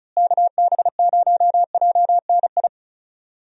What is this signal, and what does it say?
Morse code, heard as a keyed tone, about 36 wpm, 700 Hz. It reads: KB0JNI
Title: USA Call Signs - 6 Characters - Repeated 2x - 35wpm